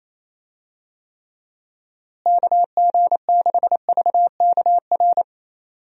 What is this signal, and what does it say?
Morse code, heard as a keyed tone, about 28 wpm, 700 Hz. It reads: KG6VKR
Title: USA Call Signs - 6 Characters - Repeated 3x - 28wpm